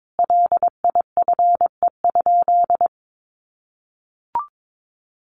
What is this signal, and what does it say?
Morse code, heard as a keyed tone, about 22 wpm, 700 Hz. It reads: LIFE?
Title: Sentences from Top 1300 Words 22wpm